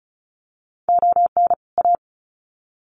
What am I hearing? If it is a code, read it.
Morse code, heard as a keyed tone, about 34 wpm, 700 Hz. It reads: ON A
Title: Sets of 2 Words 35wpm